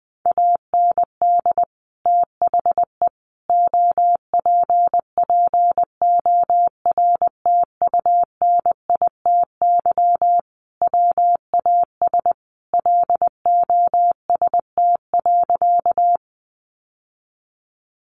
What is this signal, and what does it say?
Morse code, heard as a keyed tone, about 20 wpm, 700 Hz. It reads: AND THE OPPORTUNITY WAS LOST.
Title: Sentences from Top 1400 Words 20wpm